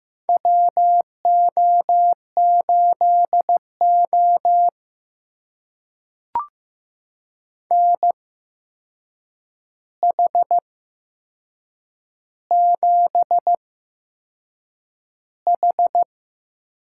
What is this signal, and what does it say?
Morse code, heard as a keyed tone, about 15 wpm, spaced as for 6 wpm, 700 Hz. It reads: WO8O NH7H
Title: USA Call Signs - 4 Characters - 8x Character Spacing - 15wpm